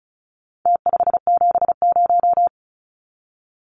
Morse code, heard as a keyed tone, about 34 wpm, 700 Hz. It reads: T570
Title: Numbers - Contest Style - 4-Digit 35wpm